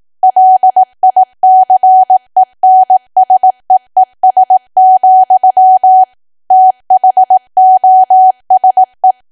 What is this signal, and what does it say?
Morse code, heard as a keyed tone, about 18 wpm, 750 Hz. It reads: LICENSEES, THOSE
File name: cwa / W1AW-Level2 / 18WPM